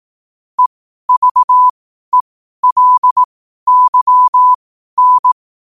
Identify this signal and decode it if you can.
Morse code, heard as a keyed tone, about 18 wpm, spaced as for 12 wpm, 1000 Hz. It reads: EVELYN
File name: qotdfiles / 12wpm